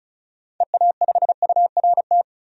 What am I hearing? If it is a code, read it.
Morse code, heard as a keyed tone, about 36 wpm, 700 Hz. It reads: EA5URT
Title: DXCC-100 Call Signs - 6 Characters - 35wpm